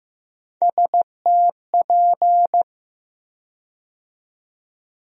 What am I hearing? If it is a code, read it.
Morse code, heard as a keyed tone, about 15 wpm, 700 Hz. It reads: STP
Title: USA Call Sign Postfixes - 3 Letters 15wpm